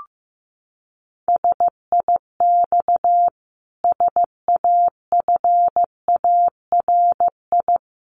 Morse code, heard as a keyed tone, about 15 wpm, 700 Hz. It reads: SIX SAFARI
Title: A Sound Of Thunder 15wpm